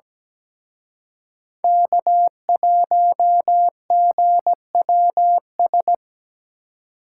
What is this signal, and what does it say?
Morse code, heard as a keyed tone, about 20 wpm, 700 Hz. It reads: K1GWS
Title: Speed Racing - USA 1x3 Call Signs 17wpm